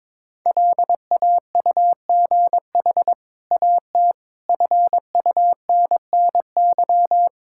Morse code, heard as a keyed tone, about 22 wpm, 700 Hz. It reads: LAUGH AT FUNNY
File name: practice / 22wpm / Easy Sentences with Hints - 22wpm